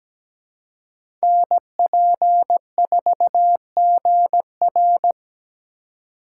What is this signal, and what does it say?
Morse code, heard as a keyed tone, about 17 wpm, 700 Hz. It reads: NP4GR
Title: USA Call Signs - 5 Characters - 4x Character Spacing - 17wpm